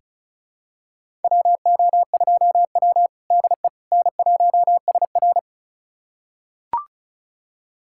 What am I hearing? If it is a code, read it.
Morse code, heard as a keyed tone, about 36 wpm, 700 Hz. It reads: WO2W DE N1SR
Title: USA Callsign DE Callsign - 4 Characters - 35wpm